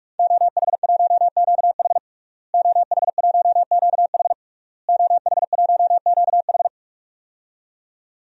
Morse code, heard as a keyed tone, about 41 wpm, 700 Hz. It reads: OH1QH OH1QH OH1QH
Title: DXCC-100 Call Signs - 5 Characters - Repeated 3x - 45wpm